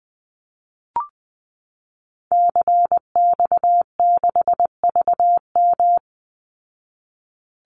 Morse code, heard as a keyed tone, about 20 wpm, 700 Hz. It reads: CX6VM
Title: Call Signs Repeated 2x - 20wpm